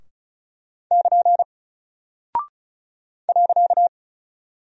Morse code, heard as a keyed tone, about 34 wpm, 700 Hz. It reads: <KN> .
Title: Single Character (Letter:Number:Punctuation:Prosign) Alt 35wpm